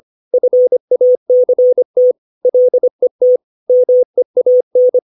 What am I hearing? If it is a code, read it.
Morse code, heard as a keyed tone, about 25 wpm, 500 Hz. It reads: FACT LET MEAN
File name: 632 words / 632 Words 25WPM